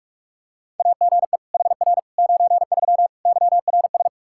Morse code, heard as a keyed tone, about 45 wpm, 700 Hz. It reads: AGE HR 93 YRS
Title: Practice QSO - Exchange 3 - More Info - 45wpm